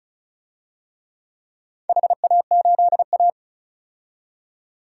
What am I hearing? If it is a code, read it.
Morse code, heard as a keyed tone, about 36 wpm, 700 Hz. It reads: HA8A
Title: DXCC-100 Call Signs - 4 Characters - 35wpm